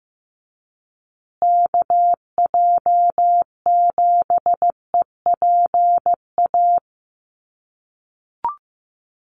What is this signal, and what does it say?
Morse code, heard as a keyed tone, about 15 wpm, 700 Hz. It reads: KJ7EPA
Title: USA Call Signs Repeated 2x 15wpm